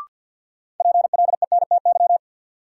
Morse code, heard as a keyed tone, about 53 wpm, 700 Hz. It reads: PLENTY
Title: Sentences from Top 1300 Words - New Word Drill - 50wpm